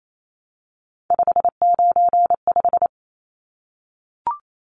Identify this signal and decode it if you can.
Morse code, heard as a keyed tone, about 28 wpm, 700 Hz. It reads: 595
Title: Numbers - 3-Digit 28wpm